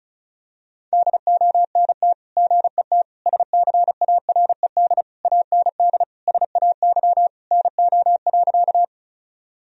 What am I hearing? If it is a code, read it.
Morse code, heard as a keyed tone, about 34 wpm, 700 Hz. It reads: DONT GET SCARED AND SAY NO.
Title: Sentences from Top 1000 Words - Review Part 4 - 35wpm